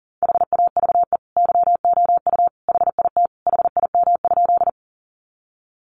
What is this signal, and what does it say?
Morse code, heard as a keyed tone, about 40 wpm, 700 Hz. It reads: HAVE YOU HIT HIM?
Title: Sentences from Top 1200 Words 40wpm